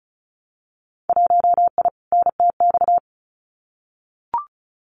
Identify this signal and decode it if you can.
Morse code, heard as a keyed tone, about 36 wpm, 700 Hz. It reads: 1I NTX
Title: Winter Field Day Exchanges 35wpm